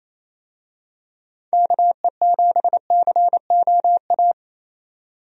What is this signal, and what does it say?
Morse code, heard as a keyed tone, about 27 wpm, 700 Hz. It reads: KE7COA
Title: USA Call Signs - 6 Characters - 28wpm